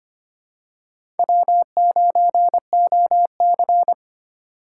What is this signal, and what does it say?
Morse code, heard as a keyed tone, about 25 wpm, 700 Hz. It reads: W9OC
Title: USA Call Signs - 4x Character Spacing - 25wpm